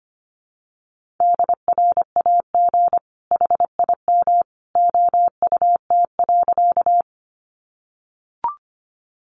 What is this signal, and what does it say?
Morse code, heard as a keyed tone, about 25 wpm, 700 Hz. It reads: DRAG HIM OUT.
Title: Sentences from Top 1500 Words 25wpm